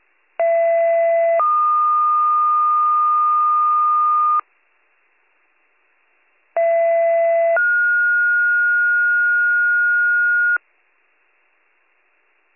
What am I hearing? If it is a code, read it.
Two-tone page, heard as a one-second tone, then a three-second tone, about 670 then 1220 Hz, about 670 then 1430 Hz. It is heard from 0.4 to 10.6 s.